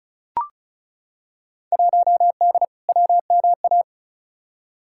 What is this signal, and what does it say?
Morse code, heard as a keyed tone, about 36 wpm, 700 Hz. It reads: 1D WMA
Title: Field Day Exchanges - 35wpm